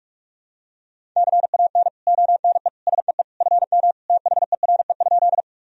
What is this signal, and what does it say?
Morse code, heard as a keyed tone, about 46 wpm, 700 Hz. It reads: CAN ONE SEE FM THERE?
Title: Sentences from Top 100 Words and Abbreviations 45wpm